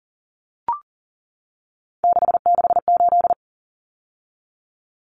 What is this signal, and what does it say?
Morse code, heard as a keyed tone, about 40 wpm, 700 Hz. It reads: 668
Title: Numbers - 3-Digit 40wpm